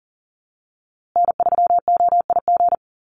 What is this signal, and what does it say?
Morse code, heard as a keyed tone, about 40 wpm, 700 Hz. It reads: N3OIG
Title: USA Call Signs Repeated 2x 40wpm